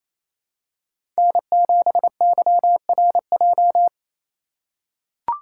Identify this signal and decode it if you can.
Morse code, heard as a keyed tone, about 28 wpm, 700 Hz. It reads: N7YRJ
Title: USA Call Signs - 5 Characters - 4x Character Spacing - 28wpm